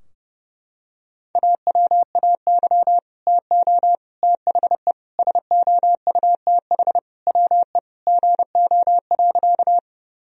Morse code, heard as a keyed tone, about 32 wpm, 700 Hz. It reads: AWAY TO THE SOUTH WE GO.
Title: Sentences from Top 1000 Words - Review Part 2 - 30wpm